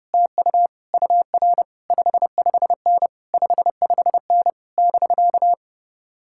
Morse code, heard as a keyed tone, about 33 wpm, 700 Hz. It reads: TU UR 55N 55N <BK>